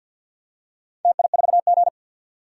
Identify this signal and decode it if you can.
Morse code, heard as a keyed tone, about 50 wpm, 700 Hz. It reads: TI4G
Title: DXCC-100 Call Signs - 4 Characters - 50wpm